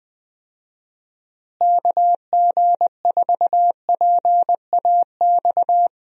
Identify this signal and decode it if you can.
Morse code, heard as a keyed tone, about 20 wpm, 700 Hz. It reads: KG4PAX
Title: USA Call Signs - 6 Characters - Repeated 2x - 20wpm